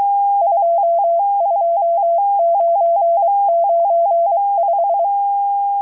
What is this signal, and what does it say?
Morse code, heard as a keyed tone, about 23 wpm, 680 Hz. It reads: 22995
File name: HM02- -0437utc-7351khz